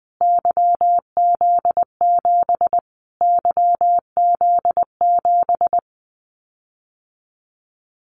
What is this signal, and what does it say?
Morse code, heard as a keyed tone, about 20 wpm, 700 Hz. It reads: YZ7 YZ7
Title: Callsign Prefixes for Top 25 DXCC - Repeated 3x - 20wpm